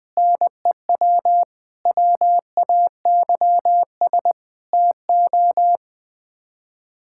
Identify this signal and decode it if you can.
Morse code, heard as a keyed tone, about 20 wpm, 700 Hz. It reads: NEW WAYS TO
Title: Sets of 6 Words - 20wpm